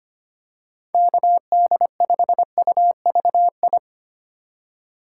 Morse code, heard as a keyed tone, about 25 wpm, 700 Hz. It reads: KD5UVI
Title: USA Call Signs - 6 Characters - 2x Character Spacing - 25wpm